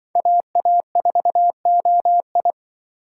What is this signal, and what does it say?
Morse code, heard as a keyed tone, about 25 wpm, 700 Hz. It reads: AA4OI
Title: Speed Racing - USA 2x2 Call Signs 20wpm